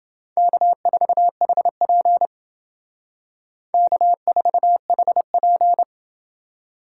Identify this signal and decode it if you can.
Morse code, heard as a keyed tone, about 28 wpm, 700 Hz. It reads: K4HP K4HP